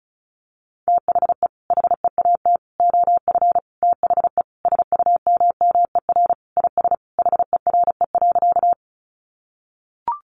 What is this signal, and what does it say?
Morse code, heard as a keyed tone, about 34 wpm, 700 Hz. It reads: THE HEAT OF THE SUMMER IS HERE.
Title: Sentences from Top 1100 Words 35wpm